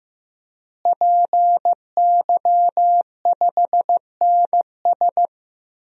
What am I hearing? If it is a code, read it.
Morse code, heard as a keyed tone, about 15 wpm, 700 Hz. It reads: PY5NS
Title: DXCC-100 Call Signs - 5 Characters - Repeated 3x - 15wpm